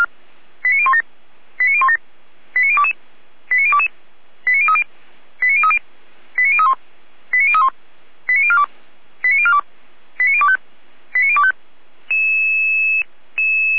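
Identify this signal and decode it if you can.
Five-tone selective call (ZVEI), heard as quick runs of five tones, tones of about 70 ms. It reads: 89017 89017 89022 89022 89030 89030 89031 89031 89042 89042 89025 89025
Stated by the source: ZVEI five-tone selective call 89012 received on an SDR.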